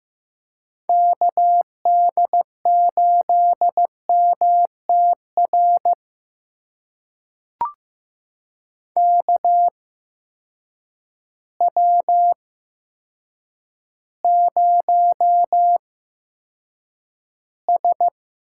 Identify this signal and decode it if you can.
Morse code, heard as a keyed tone, about 15 wpm, spaced as for 8 wpm, 700 Hz. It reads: KD8MTR KW0S
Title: USA Call Signs - 8x Character Spacing - 15wpm